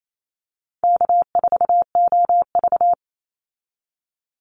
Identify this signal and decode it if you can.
Morse code, heard as a keyed tone, about 28 wpm, 700 Hz. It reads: K4OV
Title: Easy Call Signs Repeated 3x - 28wpm